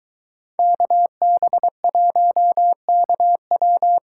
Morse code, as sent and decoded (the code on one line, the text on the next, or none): -.- -... .---- -.- .--
KB1KW